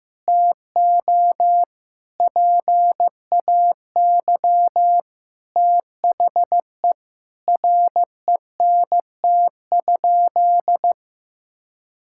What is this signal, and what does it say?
Morse code, heard as a keyed tone, about 15 wpm, 700 Hz. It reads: TO PAY THE RENT?
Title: Sentences from Top 1000 Words - Review Part 5 - 15wpm